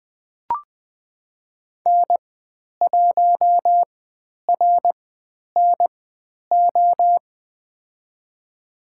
Morse code, heard as a keyed tone, about 20 wpm, spaced as for 10 wpm, 700 Hz. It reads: N1RNO
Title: 102 Lesson 17g - Letter u - Call Signs 20wpm